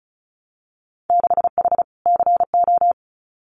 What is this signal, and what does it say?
Morse code, heard as a keyed tone, about 36 wpm, 700 Hz. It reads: 6H CO
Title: Winter Field Day Exchanges Encore 35wpm